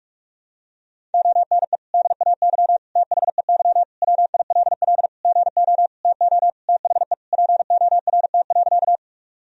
Morse code, heard as a keyed tone, about 43 wpm, 700 Hz. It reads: ONE DAY THEY WILL GO TO THE PORT.
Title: Sentences from Top 1000 Words - Review Part 1 - 45wpm